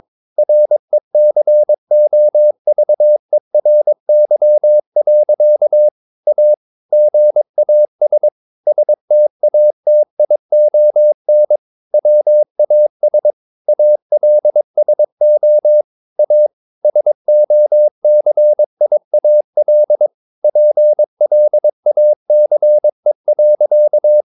Morse code, heard as a keyed tone, about 22 wpm, 600 Hz. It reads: RECOVERY. A GAS STATION WAS ALSO A SOCIAL PLACE.
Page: Romanen Never af Ken Follet på engelsk.